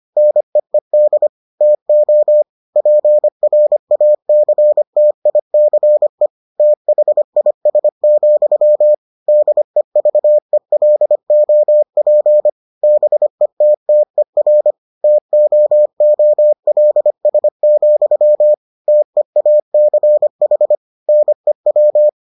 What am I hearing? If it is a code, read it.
Morse code, heard as a keyed tone, about 25 wpm, 600 Hz. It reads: NEED TO PRACTICE THIS, DEVELOP BETTER TOOLS, TEACH NEW